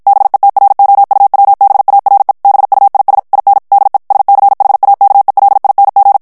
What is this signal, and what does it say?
Morse code, heard as a keyed tone, about 52 wpm, 800 Hz. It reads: BETROUWBARE BUIS IN DE SCHAKELING
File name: HOLLANDAIS texte clair 2 vitesse 50 mots minute